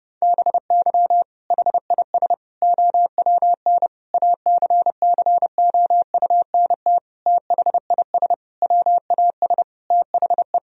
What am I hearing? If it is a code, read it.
Morse code, heard as a keyed tone, about 30 wpm, 700 Hz. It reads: BY HIS OWN ACCOUNT THIS WAS THE
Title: Sentences from Top 1000 Words - Review Part 1 - 30wpm